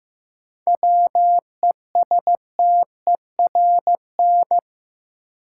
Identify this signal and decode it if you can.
Morse code, heard as a keyed tone, about 15 wpm, 700 Hz. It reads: WESTERN
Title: Sentences from Top 1000 Words - Review Part 7 - New Word Drill - 15wpm